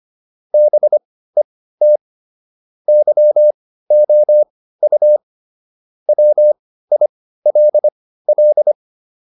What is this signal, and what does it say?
Morse code, heard as a keyed tone, about 25 wpm, spaced as for 15 wpm, 600 Hz. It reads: BET YOU WILL
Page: intermediate-practice-files